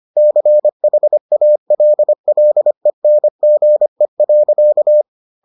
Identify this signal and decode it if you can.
Morse code, heard as a keyed tone, about 25 wpm, 600 Hz. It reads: CHALLENGE.